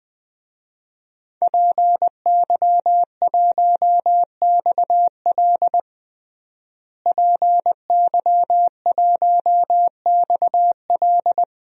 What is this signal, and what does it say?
Morse code, heard as a keyed tone, about 21 wpm, 700 Hz. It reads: PY1XL PY1XL
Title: DXCC-100 Call Signs - 5 Characters - Repeated 2x - 20wpm